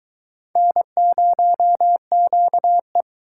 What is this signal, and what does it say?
Morse code, heard as a keyed tone, about 22 wpm, 700 Hz. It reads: N0QE